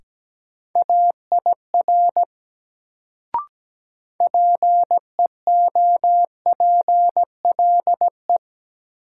Morse code, heard as a keyed tone, about 17 wpm, 700 Hz. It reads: AIR PEOPLE
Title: Sentences from Top 200 Words - New Word Drill - 17wpm